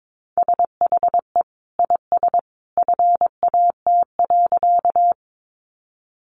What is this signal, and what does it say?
Morse code, heard as a keyed tone, about 22 wpm, 700 Hz. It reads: SHE IS FAT.
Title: Sentences from Top 600 Words 22wpm